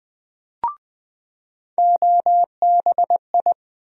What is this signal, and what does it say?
Morse code, heard as a keyed tone, about 20 wpm, 700 Hz. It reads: OBI
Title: USA Call Sign Postfixes - 3 Letters 20wpm